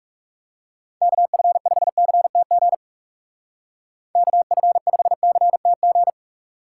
Morse code, heard as a keyed tone, about 37 wpm, 700 Hz. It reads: KF5CTG KF5CTG
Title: Speed Racing - USA 2x3 Call Signs 30wpm